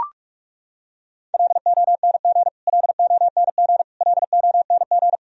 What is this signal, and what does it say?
Morse code, heard as a keyed tone, about 42 wpm, 700 Hz. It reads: LONG LONG LONG
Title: Top 100 Words Repeated 45wpm